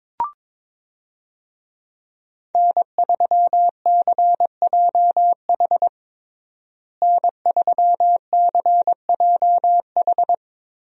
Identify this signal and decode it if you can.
Morse code, heard as a keyed tone, about 21 wpm, 700 Hz. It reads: N3CJH N3CJH
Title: USA Call Signs - 5 Characters - Repeated 2x - 22wpm